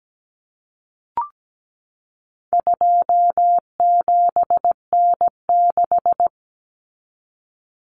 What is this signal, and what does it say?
Morse code, heard as a keyed tone, about 17 wpm, 700 Hz. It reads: 27N6
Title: Common Cut Numbers - 4-Digit 17wpm